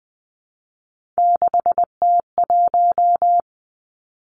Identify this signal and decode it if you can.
Morse code, heard as a keyed tone, about 19 wpm, 700 Hz. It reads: 6T1
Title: Three Challenging Letter-Number Combinations 20wpm